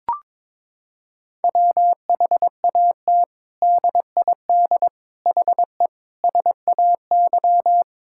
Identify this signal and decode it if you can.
Morse code, heard as a keyed tone, about 21 wpm, 700 Hz. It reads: WHAT DID HE SAY
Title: Sentences from Top 1000 Words - Review Part 6 - 22wpm